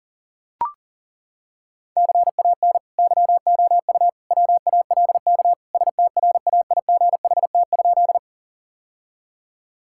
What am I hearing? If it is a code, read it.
Morse code, heard as a keyed tone, about 40 wpm, 700 Hz. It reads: CAN YOU WALK STRAIGHT?
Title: Sentences from Top 1000 Words - Review Part 6 - 40wpm